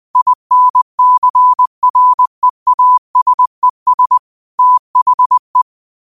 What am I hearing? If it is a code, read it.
Morse code, heard as a keyed tone, about 20 wpm, 1000 Hz. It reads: INCREASES THE
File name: qotdfiles / 20wpm